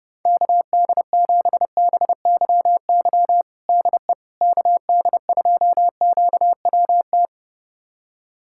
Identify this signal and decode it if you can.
Morse code, heard as a keyed tone, about 30 wpm, 700 Hz. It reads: KD7BYY DE KD2QWT
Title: USA Callsign DE Callsign - 6 Characters - 30wpm